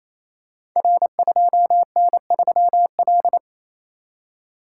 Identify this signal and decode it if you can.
Morse code, heard as a keyed tone, about 28 wpm, 700 Hz. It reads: R2N3L
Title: Code Groups - 5 Letter:Numbers - 28wpm